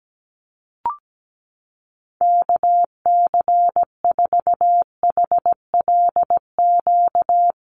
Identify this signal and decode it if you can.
Morse code, heard as a keyed tone, about 17 wpm, 700 Hz. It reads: KC4HLQ
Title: USA Call Signs Repeated 3x 17wpm